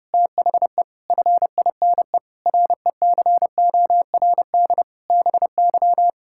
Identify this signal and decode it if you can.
Morse code, heard as a keyed tone, about 30 wpm, 700 Hz. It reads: THE FINE RECORD BY
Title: Pangram Challenge v2 - 30wpm